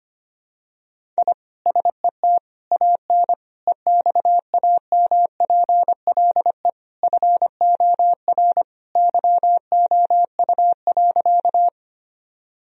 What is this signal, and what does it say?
Morse code, heard as a keyed tone, about 25 wpm, 700 Hz. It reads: I SET AN EXAMPLE FOR YOU.
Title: Sentences from Top 1000 Words - Review Part 1 - 25wpm